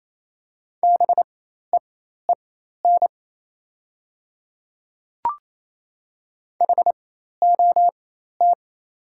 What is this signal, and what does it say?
Morse code, heard as a keyed tone, about 28 wpm, spaced as for 13 wpm, 700 Hz. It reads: BEEN HOT
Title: 4x Character Spacing - Top 100 Words - 28wpm